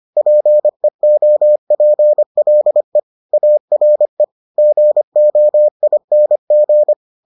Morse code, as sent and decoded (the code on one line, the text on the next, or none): .--. --- .--. .-.. / .- .-. / --. --- .. -. --.
PEOPLE ARE GOING